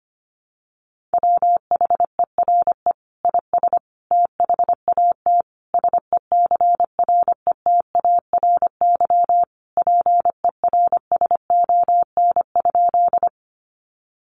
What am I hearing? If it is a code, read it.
Morse code, heard as a keyed tone, about 25 wpm, 700 Hz. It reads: WHERE IS THAT SECRETARY PERSON?
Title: Sentences from Top 1300 Words 25wpm